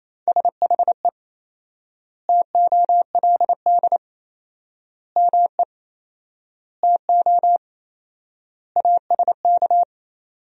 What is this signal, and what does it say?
Morse code, heard as a keyed tone, about 27 wpm, 700 Hz. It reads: SHE TOLD ME TO ASK
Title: 4x Word-Spacing - Sentences from Top 400 Words - 28wpm